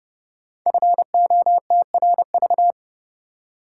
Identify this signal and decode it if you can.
Morse code, heard as a keyed tone, about 30 wpm, 700 Hz. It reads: FOTRV
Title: Code Groups - 5 Letters - 30wpm